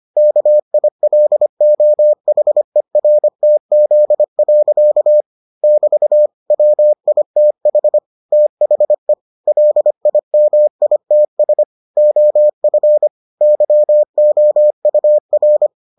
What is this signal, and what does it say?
Morse code, heard as a keyed tone, about 25 wpm, 600 Hz. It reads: KILOHERTZ. <BT> WITH THE LIMITS OF YOUR